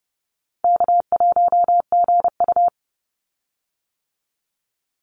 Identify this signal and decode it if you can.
Morse code, heard as a keyed tone, about 29 wpm, 700 Hz. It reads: K1GU
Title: Call Signs - 30wpm